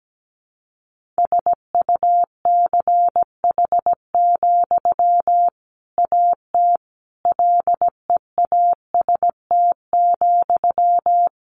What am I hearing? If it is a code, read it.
Morse code, heard as a keyed tone, about 17 wpm, 700 Hz. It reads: SUCH, AT LEAST,
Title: Sentences from Top 1500 Words 17wpm